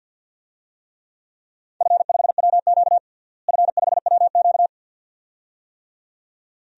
Morse code, heard as a keyed tone, about 52 wpm, 700 Hz. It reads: F5WQ F5WQ
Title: DXCC-100 Call Signs - 4 Characters - Repeated 2x - 50wpm